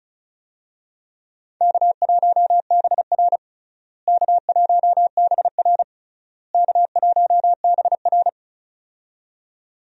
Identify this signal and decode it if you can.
Morse code, heard as a keyed tone, about 36 wpm, 700 Hz. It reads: K1BR K1BR K1BR
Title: USA Call Signs - 4 Characters - Repeated 3x - 35wpm